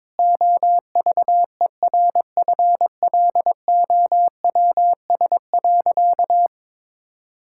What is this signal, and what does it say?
Morse code, heard as a keyed tone, about 22 wpm, 700 Hz. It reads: OVERFLOWS.
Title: Pangram Challenge v2 - 22wpm